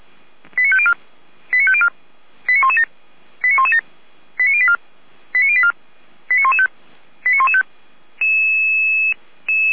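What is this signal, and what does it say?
Five-tone selective call (ZVEI), heard as quick runs of five tones, tones of about 70 ms. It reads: 89504 89504 89117 89117 89084 89084 89115 89115
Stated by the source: ZVEI five-tone selective call received on an SDR.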